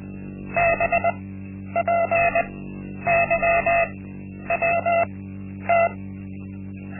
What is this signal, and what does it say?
Morse code, heard as a keyed tone, about 20 wpm, spaced as for 10 wpm, 680 Hz. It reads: BPYWT
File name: Linksmakalnis-1